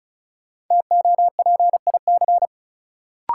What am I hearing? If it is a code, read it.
Morse code, heard as a keyed tone, about 36 wpm, 700 Hz. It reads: TOPIC
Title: Sentences from Top 1200 Words - New Word Drill Alt - 35wpm